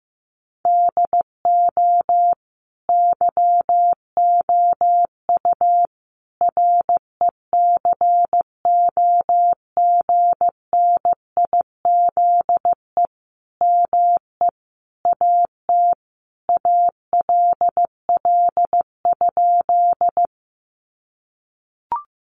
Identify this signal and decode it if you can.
Morse code, heard as a keyed tone, about 15 wpm, 700 Hz. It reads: DO YOU RECOGNIZE ME AT ALL?
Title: Sentences from Top 1400 Words 15wpm